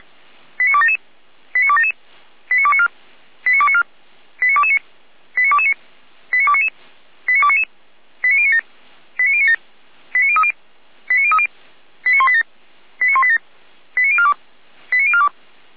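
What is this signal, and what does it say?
Five-tone selective call (ZVEI), heard as quick runs of five tones, tones of about 80 ms. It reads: 89277 89277 89294 89294 89229 89229 89299 89299 89097 89097 89030 89030 89187 89187 89042 89042
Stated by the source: ZVEI five-tone call received on SDR.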